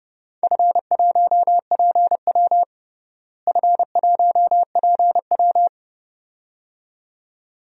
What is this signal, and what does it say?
Morse code, heard as a keyed tone, about 30 wpm, 700 Hz. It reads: F1PW F1PW
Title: DXCC-100 Call Signs - 4 Characters - Repeated 2x - 30wpm